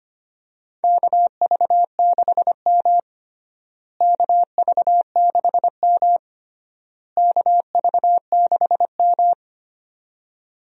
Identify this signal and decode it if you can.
Morse code, heard as a keyed tone, about 25 wpm, 700 Hz. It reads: KV6M KV6M KV6M
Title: USA Call Signs - 4 Characters - Repeated 3x - 25wpm